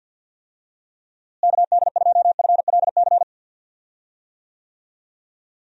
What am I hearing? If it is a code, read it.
Morse code, heard as a keyed tone, about 48 wpm, 700 Hz. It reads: KD2FLC
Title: USA Call Signs - 6 Characters - Repeated 2x - 50wpm